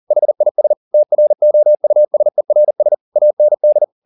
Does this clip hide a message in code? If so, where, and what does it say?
Yes: Morse code from 0.1 to 3.8 s, heard as a keyed tone, about 40 wpm, 600 Hz. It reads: HIS TROUSERS AND